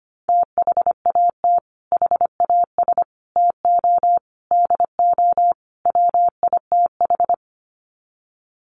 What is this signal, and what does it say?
Morse code, heard as a keyed tone, about 25 wpm, 700 Hz. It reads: THAT HAS TO DO WITH
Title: Sets of 5 Words Encore2 25wpm